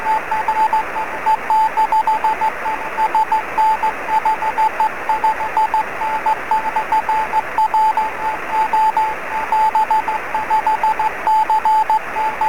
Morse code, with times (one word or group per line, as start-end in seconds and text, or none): none